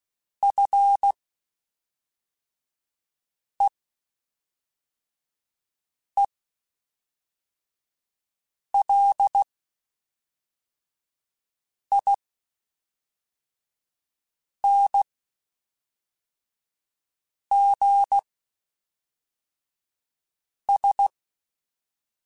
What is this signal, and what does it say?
Morse code, heard as a keyed tone, about 16 wpm, spaced as for 3 wpm, 800 Hz. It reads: FEELINGS
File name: Quotes Spaced / Quotes Spaced-mp3s / ChildAsleep sp @5